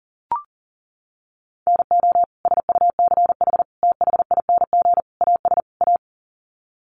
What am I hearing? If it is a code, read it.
Morse code, heard as a keyed tone, about 40 wpm, 700 Hz. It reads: NO SUCH THING AS A
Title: Sets of 5 Words Encore 40wpm